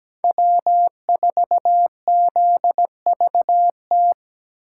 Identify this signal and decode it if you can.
Morse code, heard as a keyed tone, about 20 wpm, 700 Hz. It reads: W4ZVT
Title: Speed Racing - USA 1x3 Call Signs 17wpm